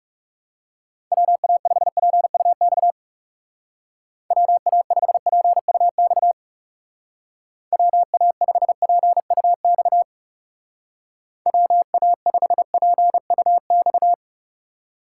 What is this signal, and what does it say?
Morse code, heard as a keyed tone, about 40 wpm, 700 Hz. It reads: WA5PUX WA5PUX WA5PUX WA5PUX